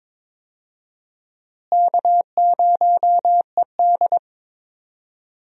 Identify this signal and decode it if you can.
Morse code, heard as a keyed tone, about 21 wpm, 700 Hz. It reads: K0ED
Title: USA Call Signs - 4 Characters - Repeated 3x - 22wpm